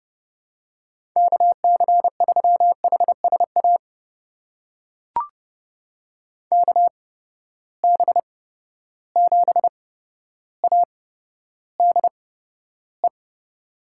Morse code, heard as a keyed tone, about 32 wpm, spaced as for 13 wpm, 700 Hz. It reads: KC3HSA KB7ADE
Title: USA Call Signs - 6 Characters - 8x Character Spacing - 30wpm